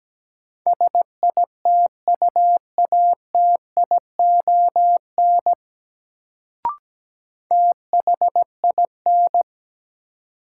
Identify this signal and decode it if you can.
Morse code, heard as a keyed tone, about 17 wpm, 700 Hz. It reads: SITUATION THIN
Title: Sentences from Top 1200 Words - New Word Drill - 17wpm